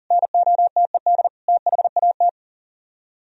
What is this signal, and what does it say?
Morse code, heard as a keyed tone, about 50 wpm, 700 Hz. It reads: NOTED THAT